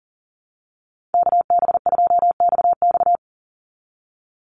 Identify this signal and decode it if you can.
Morse code, heard as a keyed tone, about 40 wpm, 700 Hz. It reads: KB2XX
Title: NA SOTA Chaser Callsigns 40wpm